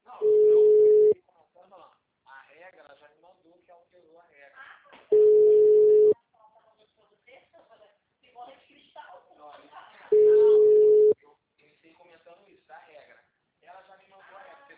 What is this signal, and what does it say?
Ringback tone (Europe, 425 Hz), with speech.
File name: out- -2033- - - .43262